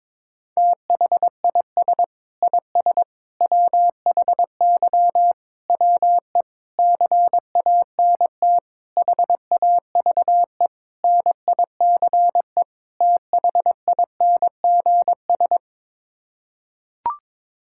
Morse code, heard as a keyed tone, about 22 wpm, 700 Hz. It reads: THIS IS WHY WE CANT HAVE NICE THINGS
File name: practice / 22wpm / Sets of 8 Words - 22wpm